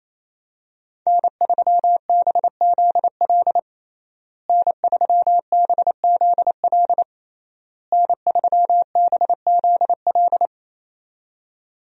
Morse code, heard as a keyed tone, about 28 wpm, 700 Hz. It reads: N3BZL N3BZL N3BZL
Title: USA Call Signs - 5 Characters - Repeated 3x - 28wpm